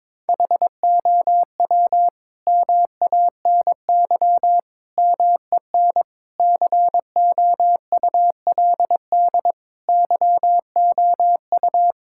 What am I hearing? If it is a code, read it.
Morse code, heard as a keyed tone, about 22 wpm, 700 Hz. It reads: HOW MANY MEN COULD YOU
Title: Sentences from Top 1600 Words 22wpm